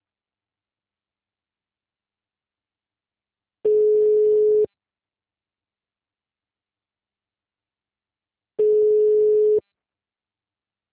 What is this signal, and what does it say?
Ringback tone (Europe, 425 Hz).
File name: out- -2004